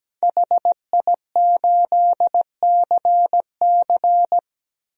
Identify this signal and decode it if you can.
Morse code, heard as a keyed tone, about 17 wpm, 700 Hz. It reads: HI8CC
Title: DXCC-100 Call Signs - 5 Characters - 17wpm